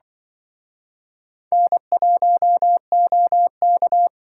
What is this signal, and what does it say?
Morse code, heard as a keyed tone, about 24 wpm, 700 Hz. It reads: N1OK
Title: Speed Racing - USA 1x2 Call Signs 20wpm